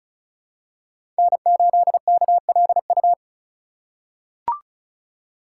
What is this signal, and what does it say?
Morse code, heard as a keyed tone, about 36 wpm, 700 Hz. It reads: N8KLU
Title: USA Call Signs - 5 Characters - 3x Character Spacing - 35wpm